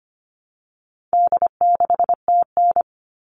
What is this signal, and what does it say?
Morse code, heard as a keyed tone, about 25 wpm, 700 Hz. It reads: D6TN
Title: All Cut Numbers - 4-Digit 25wpm